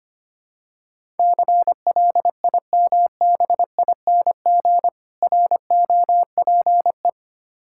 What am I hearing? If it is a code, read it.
Morse code, heard as a keyed tone, about 25 wpm, 700 Hz. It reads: CLIMBING ROPE
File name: practice / 25wpm / Difficult Words with Hints - 25wpm